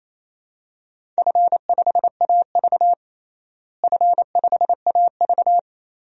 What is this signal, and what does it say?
Morse code, heard as a keyed tone, about 27 wpm, 700 Hz. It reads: F5AV F5AV
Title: DXCC-100 Call Signs - 4 Characters - Repeated 2x - 28wpm